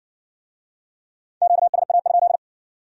Morse code, heard as a keyed tone, about 60 wpm, 700 Hz. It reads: QSA?